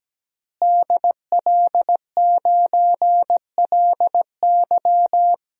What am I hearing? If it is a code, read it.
Morse code, heard as a keyed tone, about 17 wpm, 700 Hz. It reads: DL9LY
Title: DXCC-100 Call Signs - 5 Characters - Repeated 2x - 17wpm